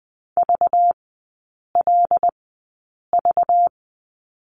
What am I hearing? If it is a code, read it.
Morse code, heard as a keyed tone, about 20 wpm, spaced as for 8 wpm, 690 Hz. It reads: VLV
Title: 191 Lesson 31e - Letter v - Three Characters 20wpm